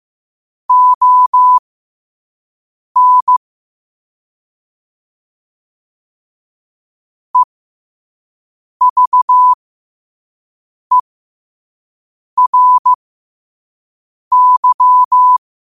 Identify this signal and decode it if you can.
Morse code, heard as a keyed tone, about 15 wpm, spaced as for 5 wpm, 1000 Hz. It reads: ON EVERY
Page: New quotes every day in morse code at 5 Words per minute.